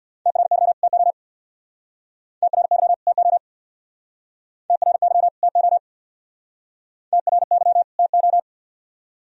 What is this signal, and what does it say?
Morse code, heard as a keyed tone, about 60 wpm, 700 Hz. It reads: TRY TO TRY TO TRY TO TRY TO
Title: Speed-Racing - Sets of 2 Words 50wpm